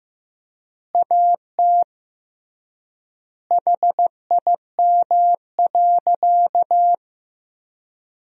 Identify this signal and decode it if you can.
Morse code, heard as a keyed tone, about 16 wpm, 700 Hz. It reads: AT HIM.
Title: 3x Word-Spacing - Sentences from Top 100 Words - 15wpm